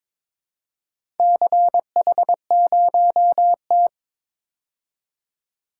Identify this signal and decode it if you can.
Morse code, heard as a keyed tone, about 23 wpm, 700 Hz. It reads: CH0T
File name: practice / 22wpm / Four Letter-Number Combinations - 22wpm